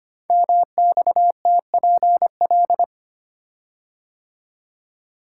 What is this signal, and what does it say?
Morse code, heard as a keyed tone, about 25 wpm, 700 Hz. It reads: MXTPL
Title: Code Groups - 5 Letters - 25wpm